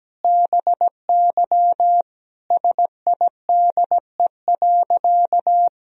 Morse code, heard as a keyed tone, about 17 wpm, 700 Hz. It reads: BY SIDE.
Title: Sentences from Top 1000 Words - Review Part 2 - 17wpm